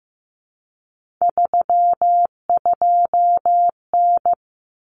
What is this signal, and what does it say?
Morse code, heard as a keyed tone, about 15 wpm, 700 Hz. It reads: 32N
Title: Common Cut Numbers - 3-Digit 15wpm